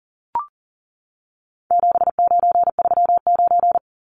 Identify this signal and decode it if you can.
Morse code, heard as a keyed tone, about 40 wpm, 700 Hz. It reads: 7939
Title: Numbers - 4-Digit 40wpm